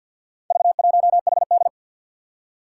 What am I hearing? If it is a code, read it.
Morse code, heard as a keyed tone, about 49 wpm, 700 Hz. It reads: V1HD
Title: Four Challenging Letter-Number Combinations 50wpm